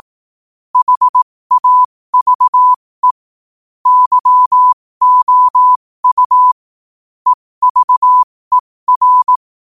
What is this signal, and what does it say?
Morse code, heard as a keyed tone, about 18 wpm, spaced as for 15 wpm, 1000 Hz. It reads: HAVE YOU EVER
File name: qotdfiles / 15wpm